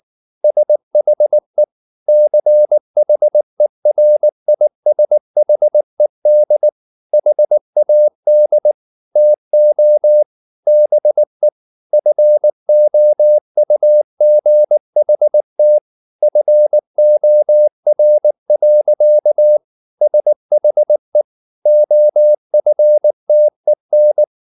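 Morse code, heard as a keyed tone, about 19 wpm, 600 Hz. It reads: SHE CHERISHED HAD TO BE FOUGHT FOR. SHE OFTEN
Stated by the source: Morse code at 19 wpm from CW med Gnister, reading Ken Follett's novel Never in English.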